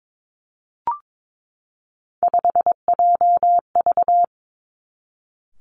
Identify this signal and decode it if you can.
Morse code, heard as a keyed tone, about 22 wpm, 700 Hz. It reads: 5JV
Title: Three Challenging Letter-Number Combinations 22wpm